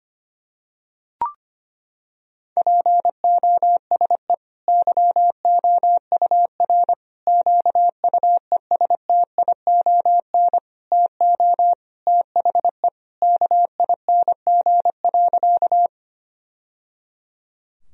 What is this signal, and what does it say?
Morse code, heard as a keyed tone, about 25 wpm, 700 Hz. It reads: POSE YOUR QUESTION TO THE KING.
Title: Sentences from Top 1000 Words - Review Part 2 - 25wpm